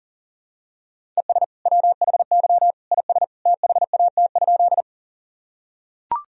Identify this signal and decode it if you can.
Morse code, heard as a keyed tone, about 40 wpm, 700 Hz. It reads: ES WHY IS THAT?
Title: Sentences from Top 200 Words and Abbreviations 40wpm